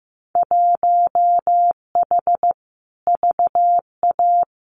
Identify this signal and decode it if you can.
Morse code, heard as a keyed tone, about 15 wpm, 700 Hz. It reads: 1H VA
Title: Winter Field Day Exchanges 15wpm